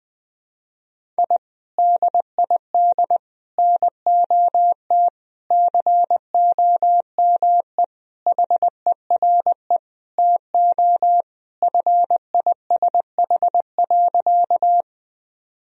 Morse code, heard as a keyed tone, about 20 wpm, 700 Hz. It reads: I DID NOT COME HERE TO FISH.
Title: Sentences from Top 1000 Words - Review Part 1 - 20wpm